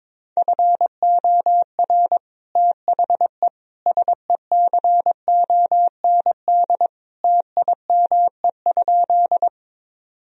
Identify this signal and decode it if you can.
Morse code, heard as a keyed tone, about 22 wpm, 700 Hz. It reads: FOR THE SECOND TIME?
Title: Sentences from Top 1000 Words - Review Part 1 - 22wpm